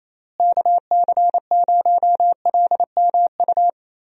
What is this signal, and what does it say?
Morse code, heard as a keyed tone, about 28 wpm, 700 Hz. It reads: KC0LMU
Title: USA Call Signs - 6 Characters - Repeated 3x - 28wpm